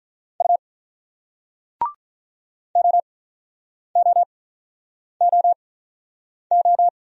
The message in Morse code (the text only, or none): U OOOO